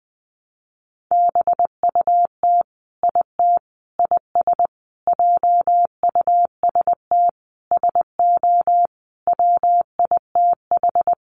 Morse code, heard as a keyed tone, about 20 wpm, 700 Hz. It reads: BUT IT IS JUST SO WITH
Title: Sentences from Top 200 Words 20wpm